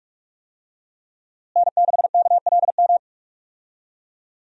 Morse code, heard as a keyed tone, about 46 wpm, 700 Hz. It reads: N6KLM
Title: USA Call Signs - 5 Characters - Repeated 2x - 45wpm